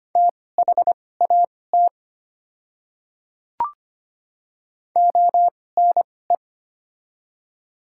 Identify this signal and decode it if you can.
Morse code, heard as a keyed tone, about 25 wpm, spaced as for 18 wpm, 700 Hz. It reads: THAT ONE
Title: 2x Character Spacing - Top 100 Words - 25wpm